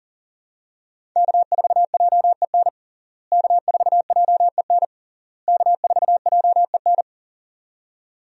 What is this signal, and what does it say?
Morse code, heard as a keyed tone, about 40 wpm, 700 Hz. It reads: K4JEN K4JEN K4JEN
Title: USA Call Signs - 5 Characters - Repeated 3x - 40wpm